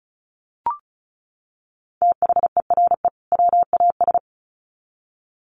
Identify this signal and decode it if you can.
Morse code, heard as a keyed tone, about 34 wpm, 700 Hz. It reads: THERE WAS
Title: Sets of 2 Words 35wpm